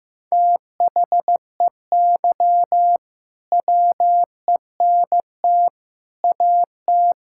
Morse code, heard as a keyed tone, about 16 wpm, 700 Hz. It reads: THEY WENT AT
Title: Sentences from Top 1000 Words - Review Part 5 - 15wpm